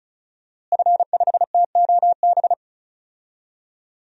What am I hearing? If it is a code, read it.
Morse code, heard as a keyed tone, about 36 wpm, 700 Hz. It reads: F5TOB
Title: DXCC-100 Call Signs - 5 Characters - Repeated 3x - 35wpm